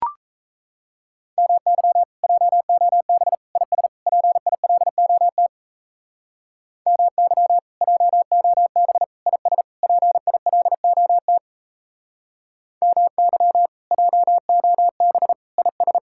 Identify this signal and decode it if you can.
Morse code, heard as a keyed tone, about 36 wpm, 700 Hz. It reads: MY JOB IS PILOT MY JOB IS PILOT MY JOB IS